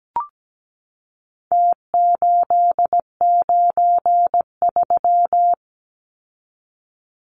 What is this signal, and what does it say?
Morse code, heard as a keyed tone, about 17 wpm, 700 Hz. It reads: T893
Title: Numbers - Contest Style - 4-Digit 17wpm